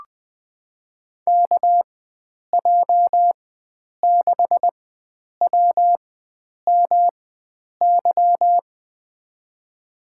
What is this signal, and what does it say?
Morse code, heard as a keyed tone, about 20 wpm, spaced as for 9 wpm, 700 Hz. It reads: KJ6WMY
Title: USA Call Signs - 6 Characters - 4x Character Spacing - 20wpm